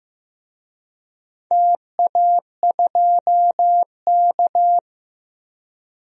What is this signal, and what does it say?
Morse code, heard as a keyed tone, about 14 wpm, 700 Hz. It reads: TA2K
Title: DXCC-100 Call Signs - 4 Characters - Repeated 2x - 15wpm